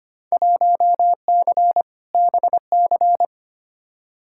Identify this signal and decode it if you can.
Morse code, heard as a keyed tone, about 25 wpm, 700 Hz. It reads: 1C BC
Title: Field Day Exchanges Encore - 25wpm